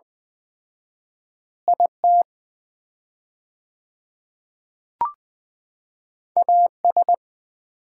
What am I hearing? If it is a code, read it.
Morse code, heard as a keyed tone, about 20 wpm, 700 Hz. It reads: IT AS
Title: 4x Word-Spacing - Sets of 2 Words - 20wpm